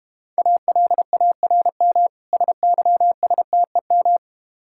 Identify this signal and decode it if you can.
Morse code, heard as a keyed tone, about 31 wpm, 700 Hz. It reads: ALARM SYSTEM
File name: Easy Words - Speed-Racing with Hints - 25wpm